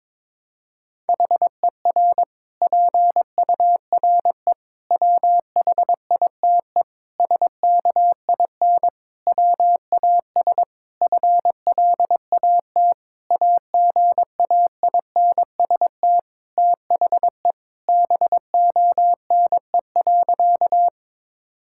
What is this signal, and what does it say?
Morse code, heard as a keyed tone, about 22 wpm, 700 Hz. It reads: HER PURE WHITE SKIN WAS FLAT AGAINST THE BONE.
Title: Sentences from Top 1600 Words 22wpm